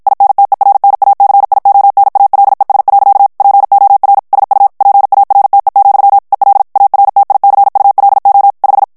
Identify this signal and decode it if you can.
Morse code, heard as a keyed tone, about 54 wpm, 800 Hz. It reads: INTERNACIONALES. POR SU PARTE, EL ARTICULO 5
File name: ESPAGNOL TEXTE EN CLAIR 2 VITESSE 50 MOTS MINUTES